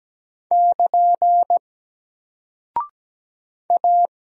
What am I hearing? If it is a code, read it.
Morse code, heard as a keyed tone, about 17 wpm, 700 Hz. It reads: <KN> A
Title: Single Character (Letter:Number:Punctuation:Prosign) Alt 17wpm